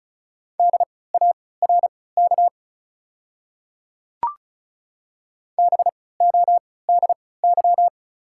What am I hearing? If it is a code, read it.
Morse code, heard as a keyed tone, about 34 wpm, spaced as for 20 wpm, 690 Hz. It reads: DARK BODY
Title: 3x Character Spacing - Top 500 Words - 35wpm